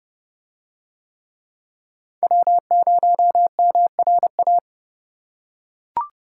Morse code, heard as a keyed tone, about 29 wpm, 700 Hz. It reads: W0MRA